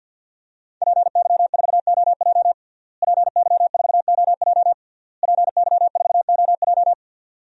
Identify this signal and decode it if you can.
Morse code, heard as a keyed tone, about 51 wpm, 700 Hz. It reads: PY4OJ PY4OJ PY4OJ
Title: DXCC-100 Call Signs - 5 Characters - Repeated 3x - 50wpm